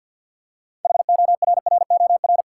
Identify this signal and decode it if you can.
Morse code, heard as a keyed tone, about 53 wpm, 700 Hz. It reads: HORROR
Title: Six Letter Words - 50wpm